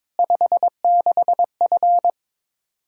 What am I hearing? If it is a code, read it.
Morse code, heard as a keyed tone, about 21 wpm, 700 Hz. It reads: 56F
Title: Practice QSO - Exchange 2 - Info - 22wpm